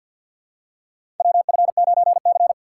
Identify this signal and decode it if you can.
Morse code, heard as a keyed tone, about 53 wpm, 700 Hz. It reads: WF9C
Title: USA Call Signs - 4 Characters - 3x Character Spacing - 50wpm